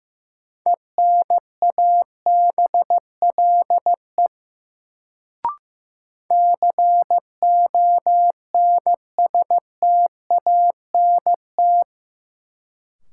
Morse code, heard as a keyed tone, about 15 wpm, 700 Hz. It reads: ENABLE CONSTANT
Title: Sentences from Top 1500 Words - New Word Drill - 15wpm